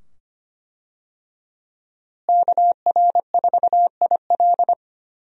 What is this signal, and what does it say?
Morse code, heard as a keyed tone, about 25 wpm, 700 Hz. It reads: KR4IL
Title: USA Call Signs - 5 Characters - Repeated 2x - 25wpm